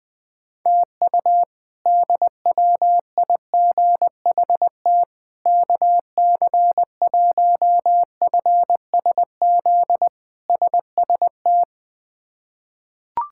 Morse code, heard as a keyed tone, about 20 wpm, 700 Hz. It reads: TU DWIGHT KC1FSZ SST
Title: K1USN SST QSO Practice - 20wpm